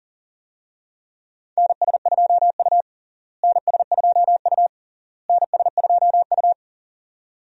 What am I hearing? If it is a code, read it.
Morse code, heard as a keyed tone, about 40 wpm, 700 Hz. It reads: NS2U NS2U NS2U
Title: USA Call Signs - 4 Characters - Repeated 3x - 40wpm